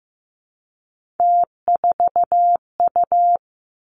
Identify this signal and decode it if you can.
Morse code, heard as a keyed tone, about 15 wpm, 700 Hz. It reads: T4U
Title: Three Challenging Letter-Number Combinations 15wpm